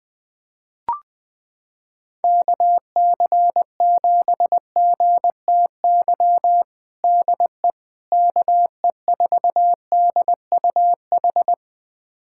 Morse code, heard as a keyed tone, about 21 wpm, 700 Hz. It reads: KC7GTY DE KE4DUH
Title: USA Callsign DE Callsign - 6 Characters - 20wpm